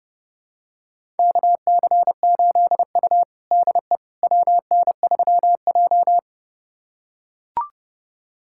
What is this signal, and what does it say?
Morse code, heard as a keyed tone, about 30 wpm, 700 Hz. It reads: KC8U DE WN3J
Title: USA Callsign DE Callsign - 4 Characters - 30wpm